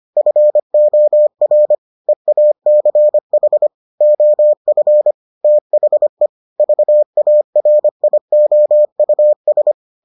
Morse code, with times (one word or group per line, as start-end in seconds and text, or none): 0.2-1.7 FOR
2.1-3.7 EACH
4.0-5.1 OF
5.4-6.3 THE
6.6-9.7 VARIOUS